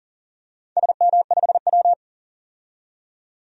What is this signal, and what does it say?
Morse code, heard as a keyed tone, about 38 wpm, 700 Hz. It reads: SM5W
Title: DXCC-100 Call Signs - 4 Characters - 40wpm